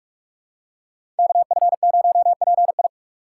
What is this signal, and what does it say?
Morse code, heard as a keyed tone, about 42 wpm, 700 Hz. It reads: KF0PI
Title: USA Call Signs - 5 Characters - 4x Character Spacing - 45wpm